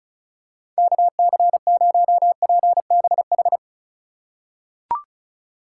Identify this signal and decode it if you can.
Morse code, heard as a keyed tone, about 34 wpm, 700 Hz. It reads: KC0PBH
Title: USA Call Signs - 2x Character Spacing - 35wpm